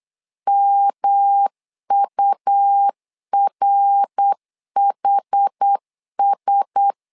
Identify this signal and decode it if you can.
Morse code, heard as a keyed tone, about 8 wpm, 800 Hz. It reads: MURHS
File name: LEZIONI / 4 Q1S 8wpm 2710